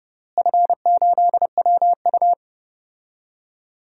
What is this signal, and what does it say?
Morse code, heard as a keyed tone, about 30 wpm, 700 Hz. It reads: F8WU
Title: DXCC-100 Call Signs - 4 Characters - Repeated 3x - 30wpm